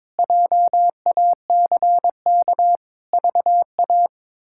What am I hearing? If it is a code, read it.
Morse code, heard as a keyed tone, about 21 wpm, 700 Hz. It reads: JACK VA
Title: K1USN SST QSO Practice - 22wpm